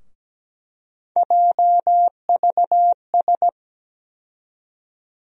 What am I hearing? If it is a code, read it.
Morse code, heard as a keyed tone, about 17 wpm, 700 Hz. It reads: JVS
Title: USA Call Sign Postfixes - 3 Letters 17wpm